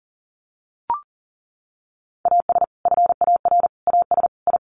Morse code, heard as a keyed tone, about 37 wpm, 700 Hz. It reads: AS FAR AS I
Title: Sets of 4 Words 40wpm